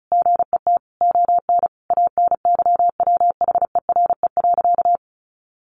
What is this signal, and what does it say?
Morse code, heard as a keyed tone, about 34 wpm, 700 Hz. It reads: GET ON ANYWHERE.
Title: Sentences from Top 1400 Words 35wpm